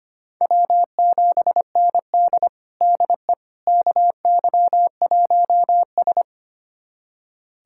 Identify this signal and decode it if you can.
Morse code, heard as a keyed tone, about 25 wpm, 700 Hz. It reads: W7ND DE KY1S
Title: USA Callsign DE Callsign - 4 Characters - 25wpm